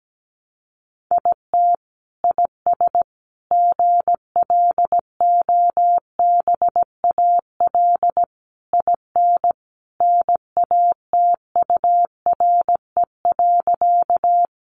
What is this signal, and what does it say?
Morse code, heard as a keyed tone, about 17 wpm, 700 Hz. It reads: IT IS GLOBAL IN NATURE.
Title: Sentences from Top 1200 Words 17wpm